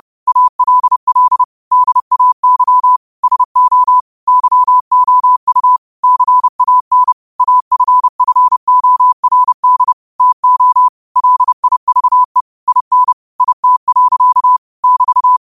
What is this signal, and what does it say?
Morse code, heard as a keyed tone, about 30 wpm, 1000 Hz. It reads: ALL DAY SO YOU CAN AFFORD TO LIVE IN IT. <BT>
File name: qotdfiles / 30wpm